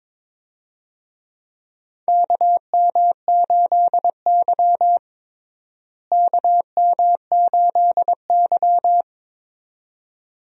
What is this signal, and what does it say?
Morse code, heard as a keyed tone, about 22 wpm, 700 Hz. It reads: KM8Y KM8Y
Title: USA Call Signs - 4 Characters - Repeated 2x - 22wpm